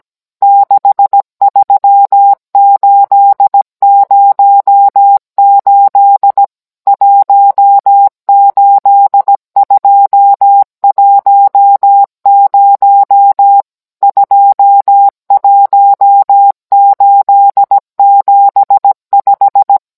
Morse code, heard as a keyed tone, about 17 wpm, 800 Hz. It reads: 63808 18210 21875